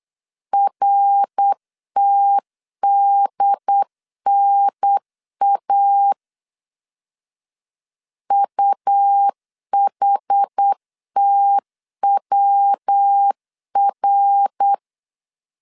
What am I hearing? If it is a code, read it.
Morse code, heard as a keyed tone, about 8 wpm, 800 Hz. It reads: RTDNA UHTWR